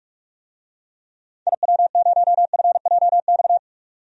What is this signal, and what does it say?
Morse code, heard as a keyed tone, about 42 wpm, 700 Hz. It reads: IW0FJX
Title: DXCC-100 Call Signs - 6 Characters - 45wpm